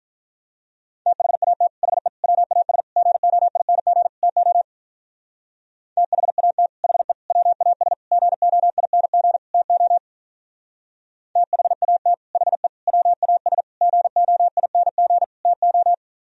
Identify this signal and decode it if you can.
Morse code, heard as a keyed tone, about 44 wpm, 700 Hz. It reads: THAT HE WAS GOING TO THAT HE WAS GOING TO THAT HE WAS GOING TO